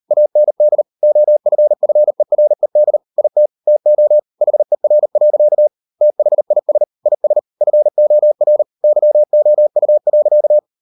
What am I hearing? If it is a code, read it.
Morse code, heard as a keyed tone, about 38 wpm, 600 Hz. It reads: AND OFFERED IT TO HER. THIS IS FOR YOU.